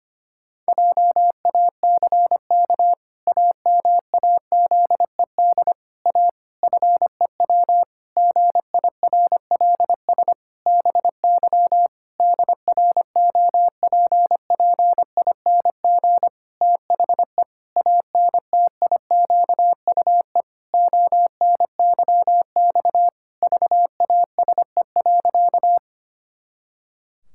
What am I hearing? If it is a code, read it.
Morse code, heard as a keyed tone, about 25 wpm, 700 Hz. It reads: JACK AMAZED A FEW GIRLS BY DROPPING THE ANTIQUE ONYX VASE.
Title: Pangram Challenge v2 - 25wpm